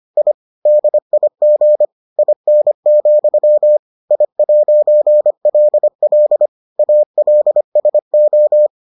Morse code, heard as a keyed tone, about 25 wpm, 600 Hz. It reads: I DIG IN, I'LL ALSO